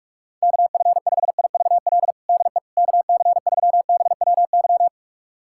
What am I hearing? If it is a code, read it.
Morse code, heard as a keyed tone, about 45 wpm, 700 Hz. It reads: KF5IVL DE KC3BWY
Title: USA Callsign DE Callsign - 6 Characters - 45wpm